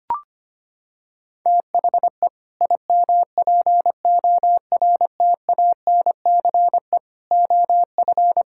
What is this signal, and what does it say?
Morse code, heard as a keyed tone, about 25 wpm, 700 Hz. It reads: THE IMPORTANCE OF
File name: practice / 25wpm / Sentences with 3-Word Phrases - 25wpm